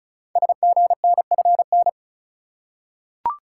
Morse code, heard as a keyed tone, about 36 wpm, 700 Hz. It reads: SGNFN
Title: Code Groups - 5 Letters - 35wpm